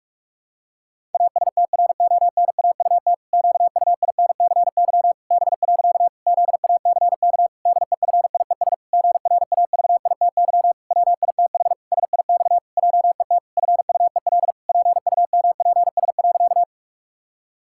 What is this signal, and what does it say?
Morse code, heard as a keyed tone, about 46 wpm, 700 Hz. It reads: ASTRONAUT QUINCY B. ZACK DEFIES GRAVITY WITH SIX JET FUEL PUMPS.
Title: Pangram Challenge v2 - 45wpm